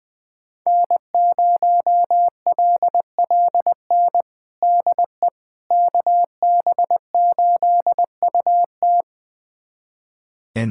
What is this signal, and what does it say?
Morse code, heard as a keyed tone, about 21 wpm, 700 Hz. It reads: N0LLN DE KB8UT
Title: USA Callsign DE Callsign - 5 Characters - 20wpm